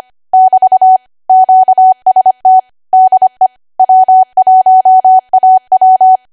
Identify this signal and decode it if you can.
Morse code, heard as a keyed tone, about 25 wpm, 750 Hz. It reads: <BT> QST DE W1AW
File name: Archive / 25 WPM / 25WPM